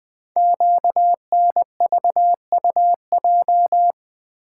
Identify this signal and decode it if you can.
Morse code, heard as a keyed tone, about 20 wpm, 700 Hz. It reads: QNVUJ
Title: Code Groups - 5 Letters - 20wpm